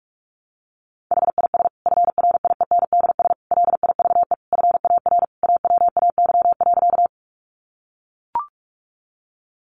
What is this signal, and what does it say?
Morse code, heard as a keyed tone, about 44 wpm, 700 Hz. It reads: HIS FRIENDS LIVE FAR AWAY.
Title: Additional Easy Phrases 45wpm